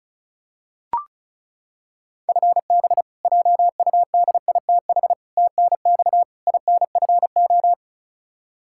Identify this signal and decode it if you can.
Morse code, heard as a keyed tone, about 36 wpm, 700 Hz. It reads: FB JUDITH TNX INFO
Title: K1USN SST QSO Practice 35wpm